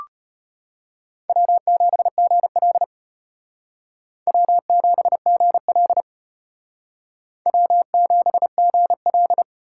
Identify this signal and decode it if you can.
Morse code, heard as a keyed tone, about 32 wpm, 700 Hz. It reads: W7GL W7GL W7GL